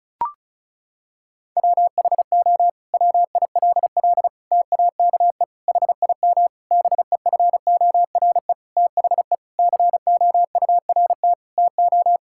Morse code, heard as a keyed tone, about 36 wpm, 700 Hz. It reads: WHO WILL TAKE HIM BEFORE THE COURT TO
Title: Sentences from Top 1600 Words 35wpm